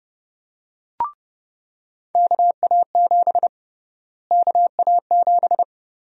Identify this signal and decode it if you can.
Morse code, heard as a keyed tone, about 30 wpm, 700 Hz. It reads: KA7 KA7
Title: USA Call Sign Prefixes - Repeated 2x - 30wpm